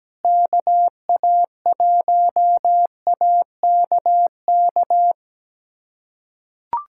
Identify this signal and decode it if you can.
Morse code, heard as a keyed tone, about 17 wpm, 700 Hz. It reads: KA1AKK
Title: USA Callsign DE Callsign - 6 Characters - 17wpm